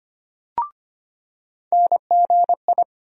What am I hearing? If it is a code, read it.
Morse code, heard as a keyed tone, about 25 wpm, 700 Hz. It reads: NGI
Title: USA Call Sign Postfixes - 3 Letters 25wpm